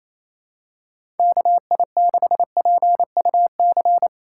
Morse code, heard as a keyed tone, about 27 wpm, 700 Hz. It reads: KI6PUC
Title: USA Call Signs - 6 Characters - 2x Character Spacing - 28wpm